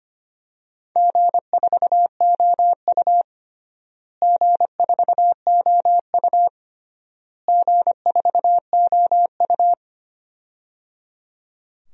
Morse code, heard as a keyed tone, about 25 wpm, 700 Hz. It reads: G4OU G4OU G4OU
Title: DXCC-100 Call Signs - 4 Characters - Repeated 3x - 25wpm